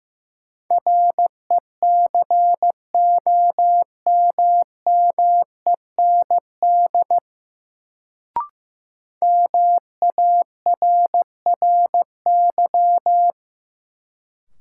Morse code, heard as a keyed tone, about 15 wpm, 700 Hz. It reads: RECOMMEND MARRY
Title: Sentences from Top 1200 Words - New Word Drill - 15wpm